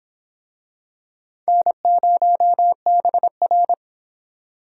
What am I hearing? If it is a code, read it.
Morse code, heard as a keyed tone, about 23 wpm, 700 Hz. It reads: N0BR